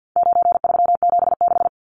Morse code, heard as a keyed tone, about 50 wpm, 700 Hz. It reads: 9376
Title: Numbers - 4-Digit 50wpm